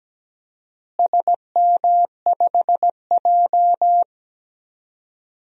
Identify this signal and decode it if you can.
Morse code, heard as a keyed tone, about 17 wpm, 700 Hz. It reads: SM5J
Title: DXCC-100 Call Signs - 4 Characters - Repeated 3x - 17wpm